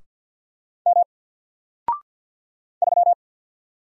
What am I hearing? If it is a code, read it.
Morse code, heard as a keyed tone, about 50 wpm, 700 Hz. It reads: M 3
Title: Single Letter-Number 50wpm